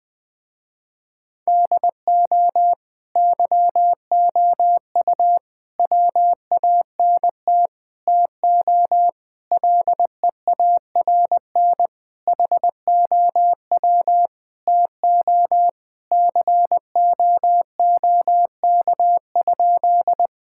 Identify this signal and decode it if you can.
Morse code, heard as a keyed tone, about 21 wpm, 700 Hz. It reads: DO YOU WANT TO LEARN HOW TO COOK?
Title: Sentences with 7-Word Phrases - Spoken Beginning - 20wpm